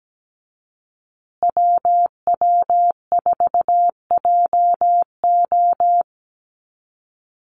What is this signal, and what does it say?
Morse code, heard as a keyed tone, about 17 wpm, 700 Hz. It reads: WW4JO
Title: USA Call Signs Repeated 2x 17wpm